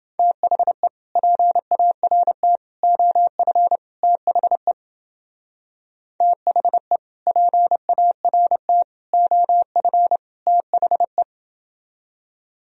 Morse code, heard as a keyed tone, about 29 wpm, 700 Hz. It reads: THE PART OF THE THE PART OF THE